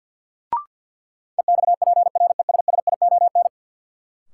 Morse code, heard as a keyed tone, about 53 wpm, 700 Hz. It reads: EXPRESSION
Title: Sentences from Top 1200 Words - New Word Drill Alt - 50wpm